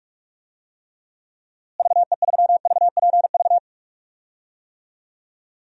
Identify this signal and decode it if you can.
Morse code, heard as a keyed tone, about 43 wpm, 700 Hz. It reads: VE3VPV
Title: DXCC-100 Call Signs - 6 Characters - Repeated 3x - 45wpm